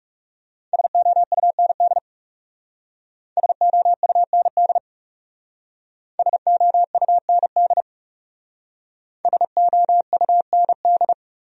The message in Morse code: SOUND SOUND SOUND SOUND

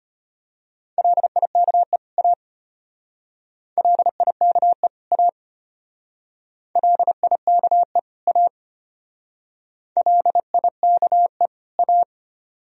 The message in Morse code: LIKE A LIKE A LIKE A LIKE A

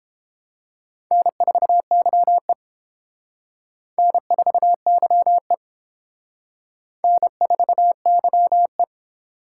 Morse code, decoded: N4YE N4YE N4YE

